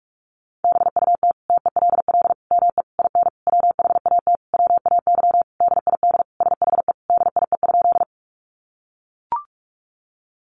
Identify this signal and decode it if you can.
Morse code, heard as a keyed tone, about 44 wpm, 700 Hz. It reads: BUT TELL ME IN WHAT WAY DID SHE DIE?
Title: Sentences from Top 1300 Words 45wpm